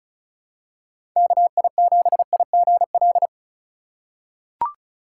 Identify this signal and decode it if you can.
Morse code, heard as a keyed tone, about 36 wpm, 700 Hz. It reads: KI7IGL
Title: USA Call Signs - 6 Characters - 4x Character Spacing - 35wpm